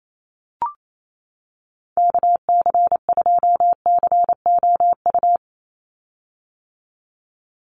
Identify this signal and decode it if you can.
Morse code, heard as a keyed tone, about 28 wpm, 700 Hz. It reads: KC2COU
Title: USA Call Signs 28wpm